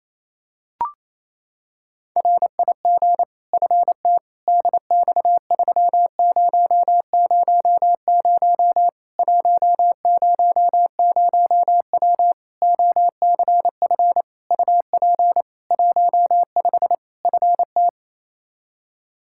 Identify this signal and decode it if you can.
Morse code, heard as a keyed tone, about 28 wpm, 700 Hz. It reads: RIG FT DX3000 100W OCF UP 15 FT
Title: K1USN SST QSO Practice 28wpm